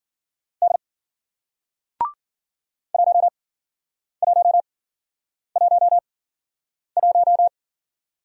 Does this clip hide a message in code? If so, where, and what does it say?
Yes: Morse code from 0.6 to 7.5 s, heard as a keyed tone, about 55 wpm, spaced as for 9 wpm, 700 Hz. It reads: D 1111